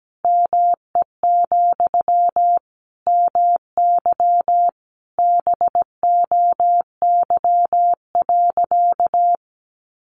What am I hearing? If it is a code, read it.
Morse code, heard as a keyed tone, about 17 wpm, 700 Hz. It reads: ME, MY BOY.
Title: Sentences from Top 900 Words 17wpm